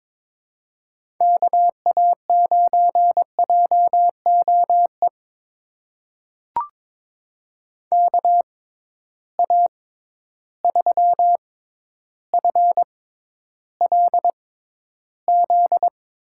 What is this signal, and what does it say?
Morse code, heard as a keyed tone, about 22 wpm, spaced as for 11 wpm, 700 Hz. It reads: KA9JOE KA3FLZ
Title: USA Call Signs - 6 Characters - 6x Character Spacing - 22wpm